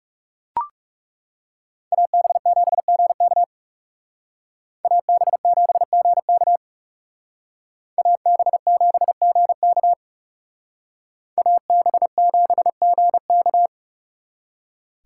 Morse code, heard as a keyed tone, about 38 wpm, 700 Hz. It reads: AB7GK AB7GK AB7GK AB7GK